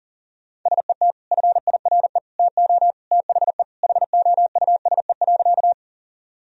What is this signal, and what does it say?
Morse code, heard as a keyed tone, about 40 wpm, 700 Hz. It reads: SET FIRE TO THE HOUSE.
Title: Sentences from Top 1000 Words - Review Part 1 - 40wpm